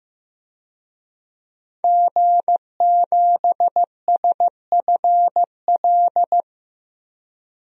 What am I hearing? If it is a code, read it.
Morse code, heard as a keyed tone, about 16 wpm, 700 Hz. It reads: G7SFL
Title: DXCC-100 Call Signs - 5 Characters - 15wpm